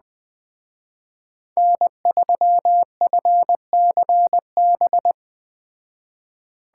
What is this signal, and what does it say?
Morse code, heard as a keyed tone, about 19 wpm, 700 Hz. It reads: N3FCB